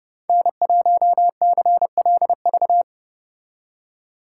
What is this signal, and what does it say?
Morse code, heard as a keyed tone, about 30 wpm, 700 Hz. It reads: N1CLV